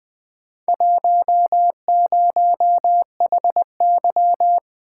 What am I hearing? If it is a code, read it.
Morse code, heard as a keyed tone, about 20 wpm, 700 Hz. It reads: 10HY
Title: Four Letter-Number Combinations - 20wpm